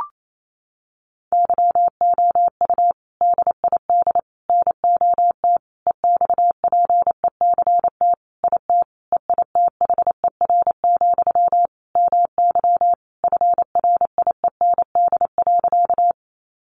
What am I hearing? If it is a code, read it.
Morse code, heard as a keyed tone, about 28 wpm, 700 Hz. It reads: YOU DID NOT EXPECT IT EITHER, MY FRIEND.
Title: Sentences from Top 1100 Words 28wpm